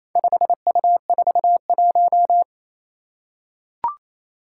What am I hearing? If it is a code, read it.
Morse code, heard as a keyed tone, about 27 wpm, 700 Hz. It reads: 5U41
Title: Four Challenging Letter-Number Combinations 28wpm